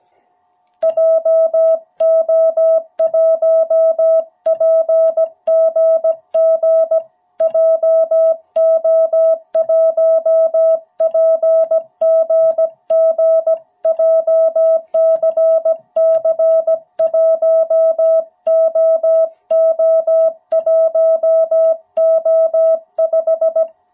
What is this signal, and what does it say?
Morse code, heard as a keyed tone, about 17 wpm, 660 Hz. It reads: JO1PGG JO1PGG JCC1OO1O5
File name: jh8cbh / 2021cw